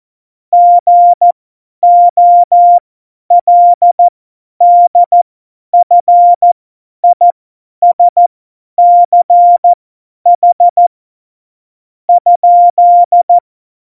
Morse code, heard as a keyed tone, about 14 wpm, spaced as for 10 wpm, 700 Hz. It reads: GOLDFISCH ?
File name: Morsekurs / Witze / Witz 16. Oktober 2018, 50 F70